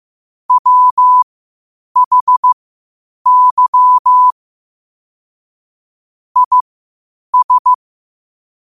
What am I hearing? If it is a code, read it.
Morse code, heard as a keyed tone, about 15 wpm, spaced as for 8 wpm, 1000 Hz. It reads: WHY IS
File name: qotdfiles / 8wpm